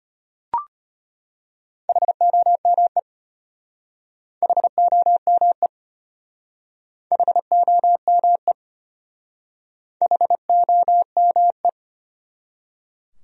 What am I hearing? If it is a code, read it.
Morse code, heard as a keyed tone, about 33 wpm, 700 Hz. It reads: HOME HOME HOME HOME